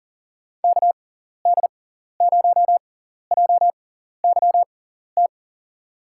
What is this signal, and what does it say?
Morse code, heard as a keyed tone, about 40 wpm, spaced as for 14 wpm, 700 Hz. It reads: KD0JYT
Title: USA Call Signs - 6 Characters - 6x Character Spacing - 40wpm